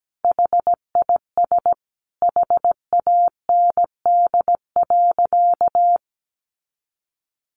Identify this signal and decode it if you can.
Morse code, heard as a keyed tone, about 17 wpm, 700 Hz. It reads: HIS HAND.
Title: A Sound Of Thunder 17wpm